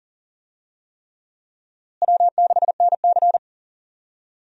Morse code, heard as a keyed tone, about 37 wpm, 700 Hz. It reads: W6NC
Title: USA Call Signs - 4 Characters - Repeated 3x - 40wpm